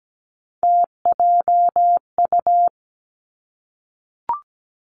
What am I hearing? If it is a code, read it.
Morse code, heard as a keyed tone, about 17 wpm, 700 Hz. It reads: TJU
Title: Three Challenging Letter-Number Combinations 17wpm